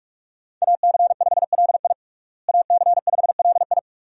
Morse code, heard as a keyed tone, about 42 wpm, 700 Hz. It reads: AC5LI AC5LI
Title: USA Call Signs - 5 Characters - Repeated 2x - 45wpm